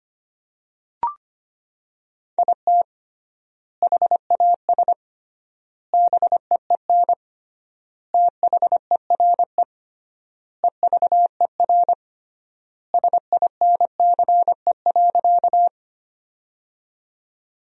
Morse code, heard as a keyed tone, about 25 wpm, 700 Hz. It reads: IT HAS BEEN THERE EVER SINCE.
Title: 3x Word-Spacing - Sentences from Top 400 Words - 25wpm